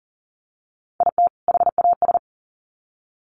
Morse code, heard as a keyed tone, about 40 wpm, 700 Hz. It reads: IT HAS
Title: Sets of 2 Words 40wpm